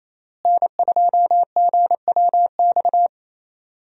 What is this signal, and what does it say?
Morse code, heard as a keyed tone, about 28 wpm, 700 Hz. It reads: N2GWX
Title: USA Call Signs - 5 Characters - Repeated 2x - 28wpm